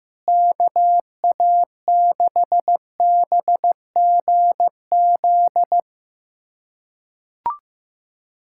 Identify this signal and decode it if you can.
Morse code, heard as a keyed tone, about 15 wpm, 700 Hz. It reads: KA6BGZ
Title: USA Callsign DE Callsign - 6 Characters - 15wpm